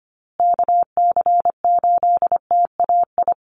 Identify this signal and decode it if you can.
Morse code, heard as a keyed tone, about 25 wpm, 700 Hz. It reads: KC8TAI
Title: USA Call Signs Repeated 3x 25wpm